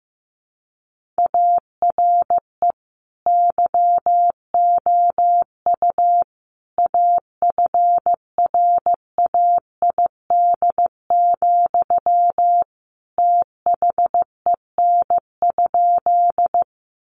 Morse code, heard as a keyed tone, about 15 wpm, 700 Hz. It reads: ARE YOU AFRAID, THEN?
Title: Sentences from Top 1500 Words 15wpm